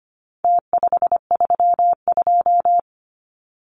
Morse code, heard as a keyed tone, about 25 wpm, 700 Hz. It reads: T532
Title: Numbers - Contest Style - 4-Digit 25wpm